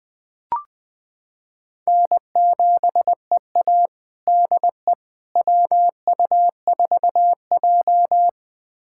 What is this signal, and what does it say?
Morse code, heard as a keyed tone, about 20 wpm, 700 Hz. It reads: N7EA DE WU4J
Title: USA Callsign DE Callsign - 4 Characters - 20wpm